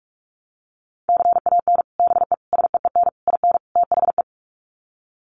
Morse code, heard as a keyed tone, about 44 wpm, 700 Hz. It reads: CAN BE SEEN IN THE
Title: Sets of 5 Words Encore2 45wpm